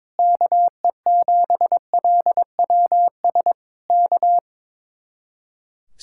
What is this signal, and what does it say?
Morse code, heard as a keyed tone, about 21 wpm, 700 Hz. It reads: KE7LWS K
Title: Speed-Racing Common QSO Phrases - 22wpm